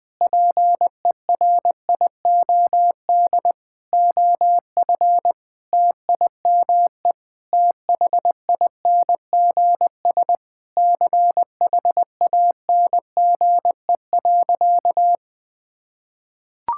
Morse code, heard as a keyed tone, about 20 wpm, 700 Hz. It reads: PERIOD OF TIME THINGS CHANGE.
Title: Sentences with 6-Word Phrases - Spoken Beginning - 20wpm